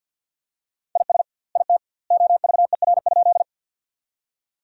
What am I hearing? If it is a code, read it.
Morse code, heard as a keyed tone, about 50 wpm, 700 Hz. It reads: IS IT OVER?
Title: 2x Word-Spacing - Sentences from Top 100 Words - 50wpm